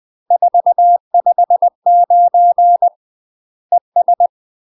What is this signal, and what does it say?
Morse code, heard as a keyed tone, about 20 wpm, 700 Hz. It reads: 459 ES
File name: UK MASTER QSOs TRAINING / 20WPM / UK MASTER QSOs 0045